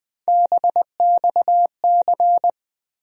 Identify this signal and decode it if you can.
Morse code, heard as a keyed tone, about 20 wpm, 700 Hz. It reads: BXC
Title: USA Call Sign Postfixes - 3 Letters 20wpm